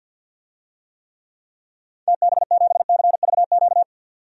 Morse code, heard as a keyed tone, about 52 wpm, 700 Hz. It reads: TB7CVQ
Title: DXCC-100 Call Signs - 6 Characters - 50wpm